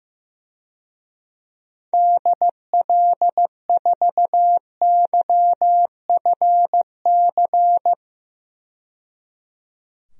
Morse code, heard as a keyed tone, about 15 wpm, 700 Hz. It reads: DL4YFC
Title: DXCC-100 Call Signs - 6 Characters - 15wpm